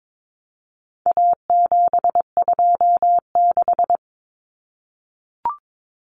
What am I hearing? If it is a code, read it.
Morse code, heard as a keyed tone, about 22 wpm, 700 Hz. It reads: A726
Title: Common Cut Numbers - 4-Digit 22wpm